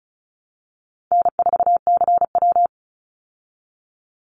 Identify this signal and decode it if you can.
Morse code, heard as a keyed tone, about 34 wpm, 700 Hz. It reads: N4CW
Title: Easy Call Signs Repeated 3x - 35wpm